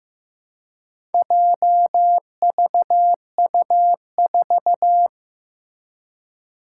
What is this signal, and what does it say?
Morse code, heard as a keyed tone, about 15 wpm, 700 Hz. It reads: JVU4
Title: Four Challenging Letter-Number Combinations 15wpm